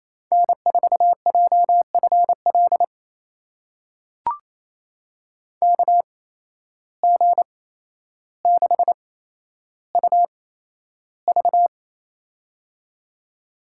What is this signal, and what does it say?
Morse code, heard as a keyed tone, about 28 wpm, spaced as for 12 wpm, 700 Hz. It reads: N4JFL KG6UV
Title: USA Call Signs - 5 Characters - 8x Character Spacing - 28wpm